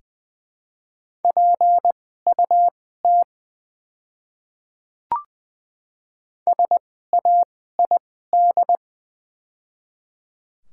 Morse code, heard as a keyed tone, about 20 wpm, spaced as for 14 wpm, 700 Hz. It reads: PUT SAID
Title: 2x Character Spacing - Top 200 Words - 20wpm